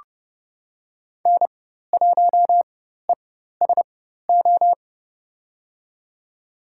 Morse code, heard as a keyed tone, about 30 wpm, spaced as for 14 wpm, 700 Hz. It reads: N1ESO
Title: USA Call Signs - 5 Characters - 4x Character Spacing - 30wpm